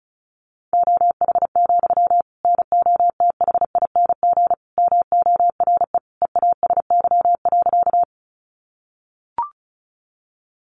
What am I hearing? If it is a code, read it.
Morse code, heard as a keyed tone, about 34 wpm, 700 Hz. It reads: OH, NOTHING MORE EASY.
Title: Sentences from Top 700 Words 35wpm